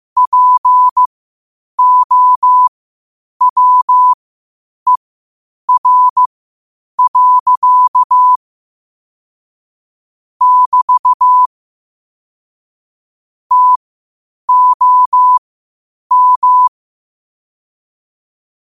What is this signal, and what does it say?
Morse code, heard as a keyed tone, about 15 wpm, spaced as for 8 wpm, 1000 Hz. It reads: POWER. <BT> TOM
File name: qotdfiles / 8wpm